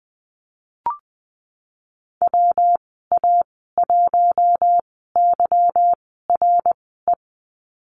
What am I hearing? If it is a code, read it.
Morse code, heard as a keyed tone, about 20 wpm, spaced as for 15 wpm, 700 Hz. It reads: WA1YRE
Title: 186 Lesson 30g - Letter b - Call Signs 20wpm